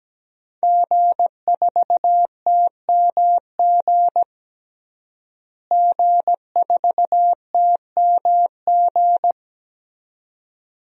Morse code, heard as a keyed tone, about 17 wpm, 700 Hz. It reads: G4TMG G4TMG
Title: DXCC-100 Call Signs - 5 Characters - Repeated 2x - 17wpm